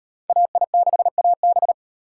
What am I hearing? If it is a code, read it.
Morse code, heard as a keyed tone, about 38 wpm, 700 Hz. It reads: AI6AB